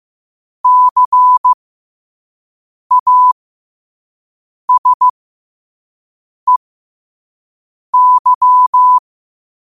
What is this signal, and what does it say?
Morse code, heard as a keyed tone, about 15 wpm, spaced as for 5 wpm, 1000 Hz. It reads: CASEY
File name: qotdfiles / 5wpm